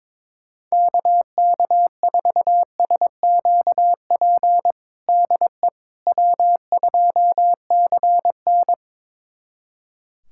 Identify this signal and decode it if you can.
Morse code, heard as a keyed tone, about 22 wpm, 700 Hz. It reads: KK4SQP DE W2CN
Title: USA Callsign DE Callsign 22wpm